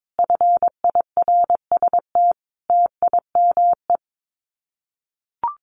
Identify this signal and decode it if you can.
Morse code, heard as a keyed tone, about 22 wpm, 700 Hz. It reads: FIRST TIME
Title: Common QSO Phrases 22wpm